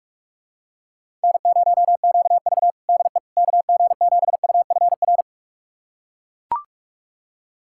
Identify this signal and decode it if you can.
Morse code, heard as a keyed tone, about 41 wpm, 700 Hz. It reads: N0QV DE KG7UFR
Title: USA Callsign DE Callsign 45wpm